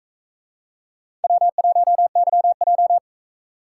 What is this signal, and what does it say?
Morse code, heard as a keyed tone, about 44 wpm, 700 Hz. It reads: W1YJ